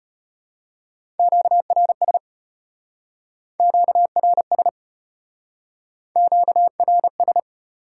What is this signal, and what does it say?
Morse code, heard as a keyed tone, about 33 wpm, 700 Hz. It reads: QRS QRS QRS